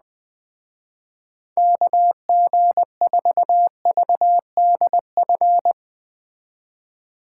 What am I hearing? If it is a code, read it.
Morse code, heard as a keyed tone, about 20 wpm, 700 Hz. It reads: KG4VDF
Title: Speed Racing - USA 2x3 Call Signs 17wpm